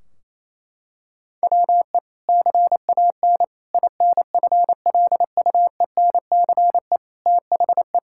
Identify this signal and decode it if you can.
Morse code, heard as a keyed tone, about 28 wpm, 700 Hz. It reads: WE CAN INFLUENCE THE
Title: Sentences from Top 1600 Words 28wpm